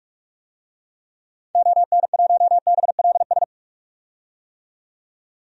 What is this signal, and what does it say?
Morse code, heard as a keyed tone, about 43 wpm, 700 Hz. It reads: ON1BLS
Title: DXCC-100 Call Signs - 6 Characters - 45wpm